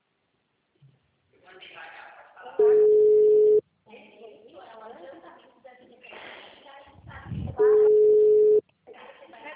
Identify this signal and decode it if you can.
Ringback tone (Europe, 425 Hz), with speech.